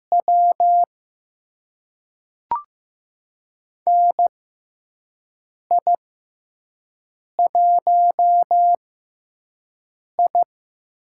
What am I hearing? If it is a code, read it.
Morse code, heard as a keyed tone, about 15 wpm, spaced as for 5 wpm, 700 Hz. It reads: W NI1I
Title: USA Call Signs - 4 Characters - 6x Character Spacing - 15wpm